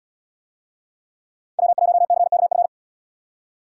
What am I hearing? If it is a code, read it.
Morse code, heard as a keyed tone, about 72 wpm, 690 Hz. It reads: W1CKF